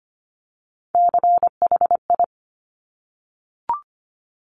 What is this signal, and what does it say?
Morse code, heard as a keyed tone, about 25 wpm, 700 Hz. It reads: CHI
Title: Top Three Letter Combinations in Words 25wpm